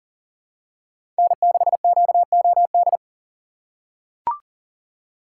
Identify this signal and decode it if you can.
Morse code, heard as a keyed tone, about 37 wpm, 700 Hz. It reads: N6QOD
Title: USA Call Signs - 3x Character Spacing - 40wpm